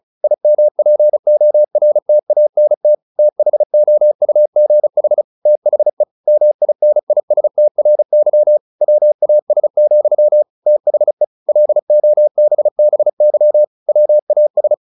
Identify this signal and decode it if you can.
Morse code, heard as a keyed tone, about 34 wpm, 600 Hz. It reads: IMPORTANT THOUGH THE MINISTRY WAS, THE LOBBY WAS